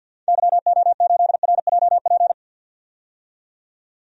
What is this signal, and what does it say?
Morse code, heard as a keyed tone, about 52 wpm, 700 Hz. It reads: YO8RJP
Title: DXCC-100 Call Signs - 6 Characters - 50wpm